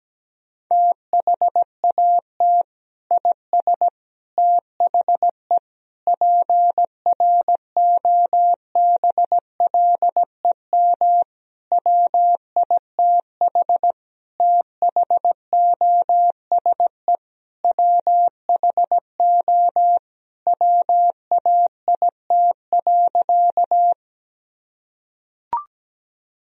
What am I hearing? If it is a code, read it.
Morse code, heard as a keyed tone, about 17 wpm, 700 Hz. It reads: THAT IS THE PROBLEM WITH THOSE WHO WAIT.
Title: Sentences from Top 1000 Words - Review Part 2 - 17wpm